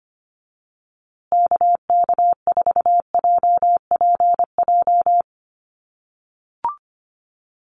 Morse code, heard as a keyed tone, about 25 wpm, 700 Hz. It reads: KK4JPJ
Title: USA Call Signs Repeated 2x 25wpm